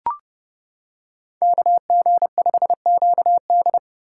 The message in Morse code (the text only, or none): KG5QD